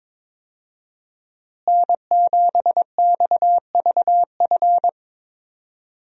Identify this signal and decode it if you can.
Morse code, heard as a keyed tone, about 22 wpm, 700 Hz. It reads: N7XVF
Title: USA Call Signs - 5 Characters - Repeated 2x - 22wpm